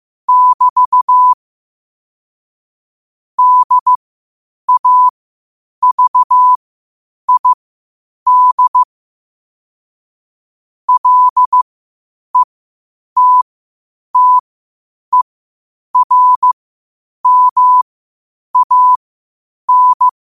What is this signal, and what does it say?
Morse code, heard as a keyed tone, about 15 wpm, spaced as for 8 wpm, 1000 Hz. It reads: <BT> DAVID LETTERMAN